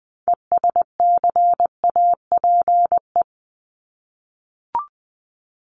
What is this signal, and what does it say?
Morse code, heard as a keyed tone, about 20 wpm, 700 Hz. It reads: ESCAPE
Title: Top 1501-1750 Words 20wpm